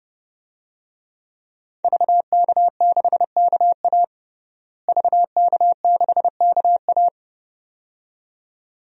Morse code, heard as a keyed tone, about 32 wpm, 700 Hz. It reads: VK6KA VK6KA
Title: DXCC-100 Call Signs - 5 Characters - Repeated 2x - 30wpm